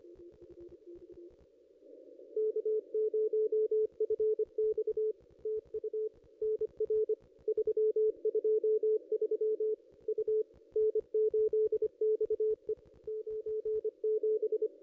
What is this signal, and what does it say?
Morse code, heard as a keyed tone, about 25 wpm, 440 Hz. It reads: K0FX TU NR 323 U N8XE 97
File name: contests / ss-cw-k5tr-op-n5rz -2200